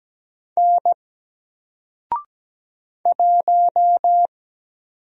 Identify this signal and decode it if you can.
Morse code, heard as a keyed tone, about 17 wpm, 700 Hz. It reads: N 1
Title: Single Character (Letter:Number:Punctuation:Prosign) Alt 17wpm